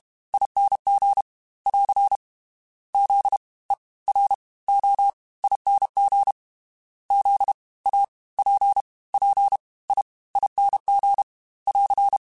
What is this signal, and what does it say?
Morse code, heard as a keyed tone, about 32 wpm, 800 Hz. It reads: ING <AR> Z E R O ING Z A P P I ING <AR>
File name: Special INGs / Special INGs-mp3s / Fast ing @30